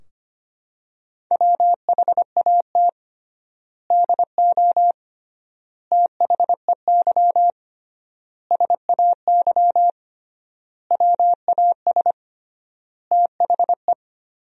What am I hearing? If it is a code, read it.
Morse code, heard as a keyed tone, about 25 wpm, 700 Hz. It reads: WHAT DO THEY SAY WAS THE
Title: 3x Word-Spacing - Sentences from Top 200 Words - 25wpm